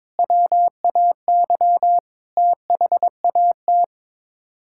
Morse code, heard as a keyed tone, about 22 wpm, 700 Hz. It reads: WAY THAT
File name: practice / 22wpm / Build A Phrase - 5-Word Phrases - 22wpm